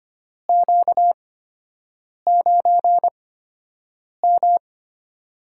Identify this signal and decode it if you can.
Morse code, heard as a keyed tone, about 25 wpm, spaced as for 7 wpm, 700 Hz. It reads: Q9M
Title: USA Call Signs - 8x Character Spacing - 25wpm